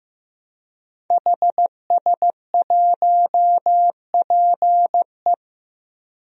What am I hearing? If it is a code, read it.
Morse code, heard as a keyed tone, about 15 wpm, 700 Hz. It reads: HS1PE
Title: DXCC-100 Call Signs - 5 Characters - 15wpm